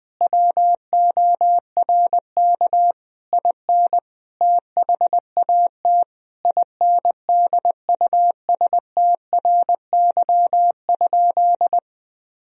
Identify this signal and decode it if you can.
Morse code, heard as a keyed tone, about 21 wpm, 700 Hz. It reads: WORK IN THAT INDUSTRY?
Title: Sentences from Top 1000 Words - Review Part 4 - 20wpm